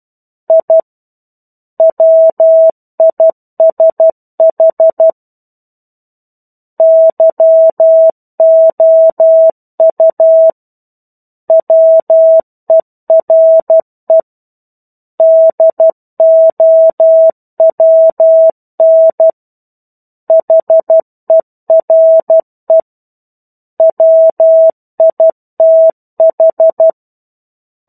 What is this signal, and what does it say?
Morse code, heard as a keyed tone, about 12 wpm, 650 Hz. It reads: I WISH YOU WERE DOWN HERE WITH
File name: 01-Chapter-01 12 WPM